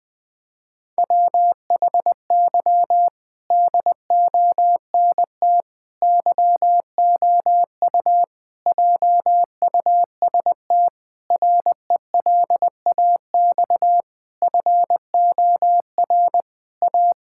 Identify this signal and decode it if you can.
Morse code, heard as a keyed tone, about 20 wpm, 700 Hz. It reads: WHY DONT YOU JUST RELAX FOR A
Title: Sentences with 5-Word Phrases - 20wpm